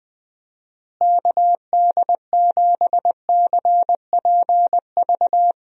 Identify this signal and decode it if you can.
Morse code, heard as a keyed tone, about 21 wpm, 700 Hz. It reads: KD7CPV
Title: USA Call Signs - 6 Characters - Repeated 2x - 20wpm